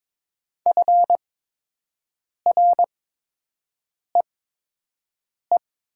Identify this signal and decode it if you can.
Morse code, heard as a keyed tone, about 22 wpm, spaced as for 6 wpm, 700 Hz. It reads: FREE
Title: 8x Character Spacing - Top 500 Words - 22wpm